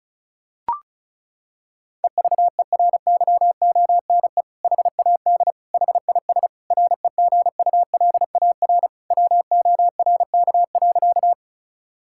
Morse code, heard as a keyed tone, about 36 wpm, 700 Hz. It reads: EVERYONE HAD HIS REGULAR WORK.
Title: Sentences from Top 1000 Words - Review Part 6 - 35wpm